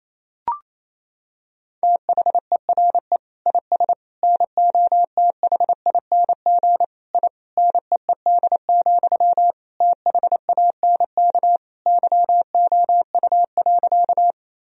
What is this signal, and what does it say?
Morse code, heard as a keyed tone, about 28 wpm, 700 Hz. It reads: THERE IS NOTHING I NEED, THANK YOU.
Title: Sentences from Top 1000 Words - Review Part 1 - 28wpm